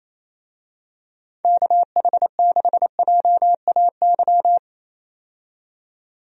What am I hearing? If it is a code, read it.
Morse code, heard as a keyed tone, about 28 wpm, 700 Hz. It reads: KH6JAY
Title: DXCC-100 Call Signs - 6 Characters - 28wpm